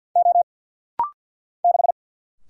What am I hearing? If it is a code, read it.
Morse code, heard as a keyed tone, about 50 wpm, 690 Hz. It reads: O 6
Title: Single Letter-Number Alt - Rapid-Fire 50wpm